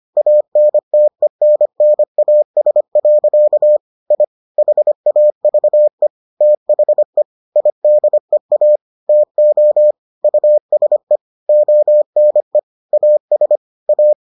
Morse code, heard as a keyed tone, about 25 wpm, 600 Hz. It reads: ANTENNAS. I HAVE THE IDEA TO USE ONE AS A